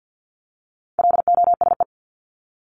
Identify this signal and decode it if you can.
Morse code, heard as a keyed tone, about 50 wpm, 700 Hz. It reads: LOSE
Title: Top 1001-1250 Words 50wpm